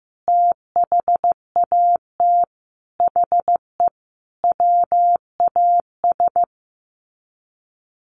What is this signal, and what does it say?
Morse code, heard as a keyed tone, about 15 wpm, 700 Hz. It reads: THAT HE WAS
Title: Sets of 5 Words 15wpm